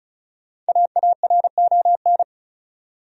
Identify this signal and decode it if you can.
Morse code, heard as a keyed tone, about 36 wpm, 700 Hz. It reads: AARON
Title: USA Common Names - Max 5 Letters - 35wpm